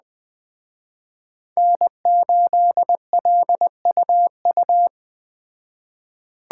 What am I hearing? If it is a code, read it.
Morse code, heard as a keyed tone, about 20 wpm, 700 Hz. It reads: N8LUU
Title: Speed Racing - USA 1x3 Call Signs 17wpm